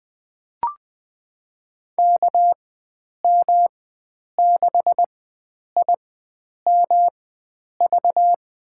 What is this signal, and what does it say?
Morse code, heard as a keyed tone, about 20 wpm, spaced as for 9 wpm, 700 Hz. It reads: KM6IMV
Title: USA Call Signs - 6 Characters - 4x Character Spacing - 20wpm